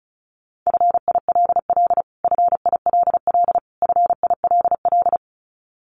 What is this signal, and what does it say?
Morse code, heard as a keyed tone, about 34 wpm, 700 Hz. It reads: FILL FILL FILL
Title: Top 500 Words Repeated 35wpm